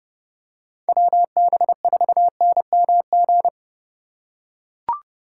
Morse code, heard as a keyed tone, about 28 wpm, 700 Hz. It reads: WB4NMG